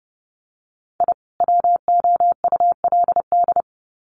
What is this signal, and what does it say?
Morse code, heard as a keyed tone, about 30 wpm, 700 Hz. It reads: I WOULD
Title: Sets of 2 Words 30wpm